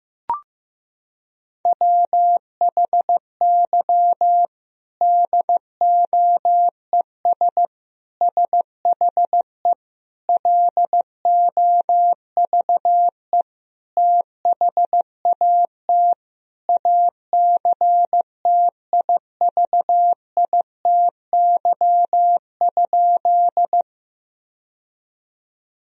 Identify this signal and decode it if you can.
Morse code, heard as a keyed tone, about 15 wpm, 700 Hz. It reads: WHY DOES SHE LOVE THAT ACTIVITY?
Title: Sentences from Top 1000 Words - Review Part 4 - 15wpm